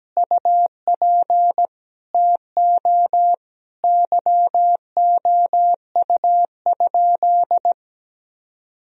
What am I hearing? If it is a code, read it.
Morse code, heard as a keyed tone, about 17 wpm, 700 Hz. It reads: UP TO YOU?
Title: Sentences from Top 1000 Words - Review Part 2 - 17wpm